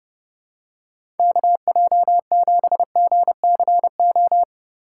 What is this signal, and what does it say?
Morse code, heard as a keyed tone, about 31 wpm, 700 Hz. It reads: KJ7GCO
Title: USA Call Signs - 6 Characters - 2x Character Spacing - 30wpm